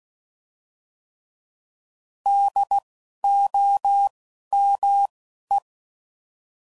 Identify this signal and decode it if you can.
Morse code, heard as a keyed tone, about 16 wpm, 800 Hz. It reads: D O M E
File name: Prefixes / By First Letter / Ds